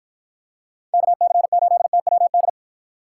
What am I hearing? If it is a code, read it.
Morse code, heard as a keyed tone, about 48 wpm, 700 Hz. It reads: KC8TWD